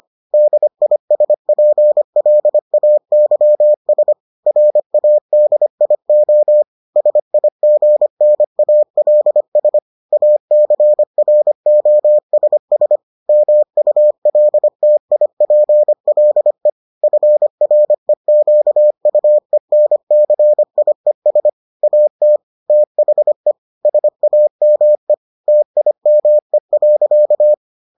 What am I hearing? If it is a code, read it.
Morse code, heard as a keyed tone, about 25 wpm, 600 Hz. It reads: DISPLAYS RADIO SIGNALS ACROSS MULTIPLE FREQUENCIES AT THE SAME TIME.